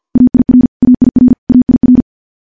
Morse code, heard as a keyed tone, about 50 wpm, 250 Hz. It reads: AIR AIR AIR
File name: morse / morse-training-50WPM